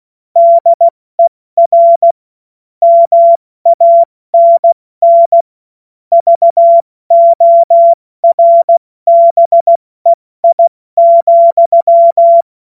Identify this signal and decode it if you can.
Morse code, heard as a keyed tone, about 16 wpm, 680 Hz. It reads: DER MANN VORBEI,
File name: Morsekurs / Witze / Witz 15. November 2022 70 F80